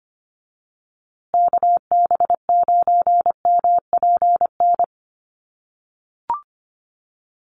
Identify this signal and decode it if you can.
Morse code, heard as a keyed tone, about 25 wpm, 700 Hz. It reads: KB9MPN
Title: USA Call Signs Repeated 3x 25wpm